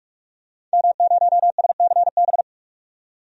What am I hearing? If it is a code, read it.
Morse code, heard as a keyed tone, about 44 wpm, 700 Hz. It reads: M0SCB
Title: DXCC-100 Call Signs - 5 Characters - 45wpm